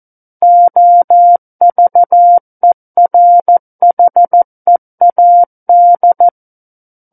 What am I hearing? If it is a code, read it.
Morse code, heard as a keyed tone, about 14 wpm, 700 Hz. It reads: OVERHEAD
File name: Book One / 05-Chapter 5 - 14 WPM